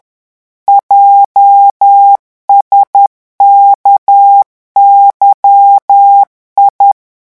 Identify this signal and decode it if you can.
Morse code, heard as a keyed tone, about 10 wpm, 800 Hz. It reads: JSKYI
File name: morse mp3 / revision11 lettres vitesse 10